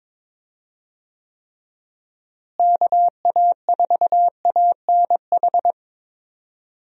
Morse code, heard as a keyed tone, about 22 wpm, 700 Hz. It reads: KA4ANH
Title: USA Call Signs - 6 Characters - Repeated 2x - 22wpm